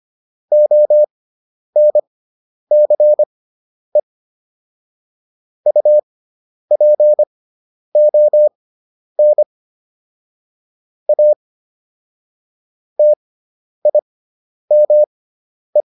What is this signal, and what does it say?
Morse code, heard as a keyed tone, about 25 wpm, spaced as for 10 wpm, 600 Hz. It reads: ONCE UPON A TIME